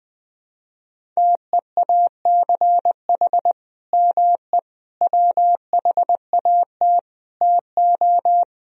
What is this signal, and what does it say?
Morse code, heard as a keyed tone, about 21 wpm, 700 Hz. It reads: TEACH ME WHAT TO
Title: Sentences from Top 500 Words and Abbreviations 20wpm